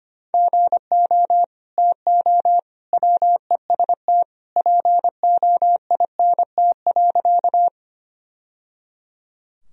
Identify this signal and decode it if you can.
Morse code, heard as a keyed tone, about 25 wpm, 700 Hz. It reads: GO TO WEST POINT.
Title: Sentences from Top 1000 Words - Review Part 2 - 25wpm